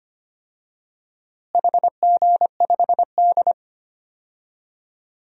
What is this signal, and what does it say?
Morse code, heard as a keyed tone, about 25 wpm, 700 Hz. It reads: HG5D
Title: DXCC-100 Call Signs - 4 Characters - Repeated 2x - 25wpm